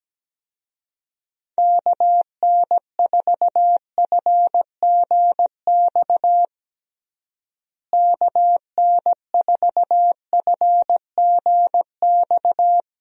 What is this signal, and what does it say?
Morse code, heard as a keyed tone, about 17 wpm, 700 Hz. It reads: KN4FGX KN4FGX
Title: USA Call Signs - 6 Characters - Repeated 2x - 17wpm